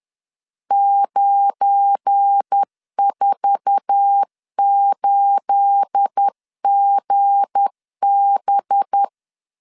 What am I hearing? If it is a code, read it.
Morse code, heard as a keyed tone, about 10 wpm, 800 Hz. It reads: 948GB
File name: LEZIONI / 10 SIGNS / 10C DOT 10wpm 2940